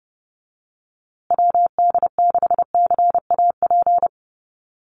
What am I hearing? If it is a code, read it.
Morse code, heard as a keyed tone, about 30 wpm, 700 Hz. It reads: WD6CAP
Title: USA Call Signs Repeated 2x 30wpm